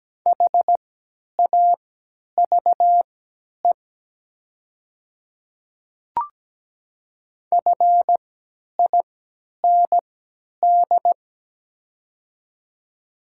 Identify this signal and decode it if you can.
Morse code, heard as a keyed tone, about 17 wpm, spaced as for 10 wpm, 700 Hz. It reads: HAVE FIND
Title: 3x Character Spacing - Top 100 Words - 17wpm